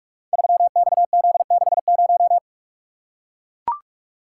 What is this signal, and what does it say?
Morse code, heard as a keyed tone, about 44 wpm, 700 Hz. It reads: 3XZ60
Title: Code Groups - 5 Letter:Numbers - 45wpm